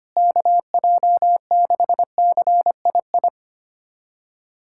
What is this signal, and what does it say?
Morse code, heard as a keyed tone, about 25 wpm, 700 Hz. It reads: KJ6CII
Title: USA Callsign DE Callsign - 6 Characters - 25wpm